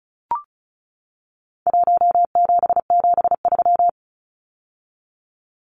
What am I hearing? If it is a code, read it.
Morse code, heard as a keyed tone, about 34 wpm, 700 Hz. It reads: 1773
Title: Numbers - Contest Style - 4-Digit 35wpm